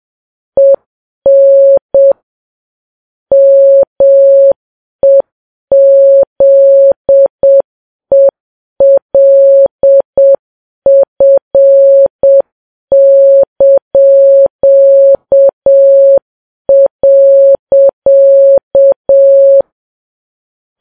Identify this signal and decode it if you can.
Morse code, heard as a keyed tone, about 7 wpm, 550 Hz. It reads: EN MEZELF).
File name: Files Web tls / Morse on9fv / morse7kps